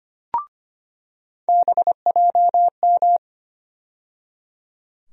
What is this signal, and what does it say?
Morse code, heard as a keyed tone, about 25 wpm, 700 Hz. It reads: BJM
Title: USA Call Sign Postfixes - 3 Letters 25wpm